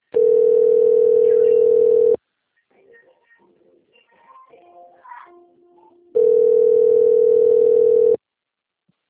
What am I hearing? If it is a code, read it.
Ringback tone (North America), 440 + 480 Hz.